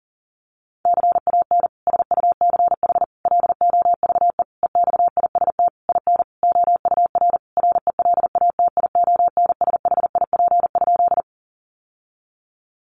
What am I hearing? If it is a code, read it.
Morse code, heard as a keyed tone, about 40 wpm, 700 Hz. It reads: CAN SUCH LOVE EXIST IN OUR RELATIONSHIP?
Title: Sentences from Top 1100 Words 40wpm